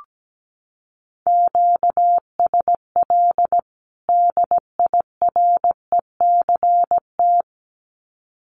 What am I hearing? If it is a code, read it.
Morse code, heard as a keyed tone, about 17 wpm, 700 Hz. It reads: QSL DIRECT
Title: Common QSO Phrases 17wpm